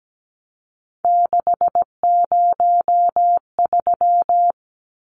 Morse code, heard as a keyed tone, about 17 wpm, 700 Hz. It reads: 603
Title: Numbers - 3-Digit 17wpm